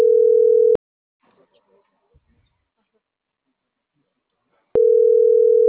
Ringback tone (North America), 440 + 480 Hz.